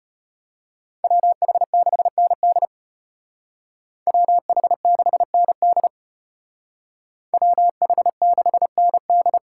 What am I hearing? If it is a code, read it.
Morse code, heard as a keyed tone, about 30 wpm, 700 Hz. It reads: WH6ND WH6ND WH6ND
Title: Speed Racing - USA Call Signs 25wpm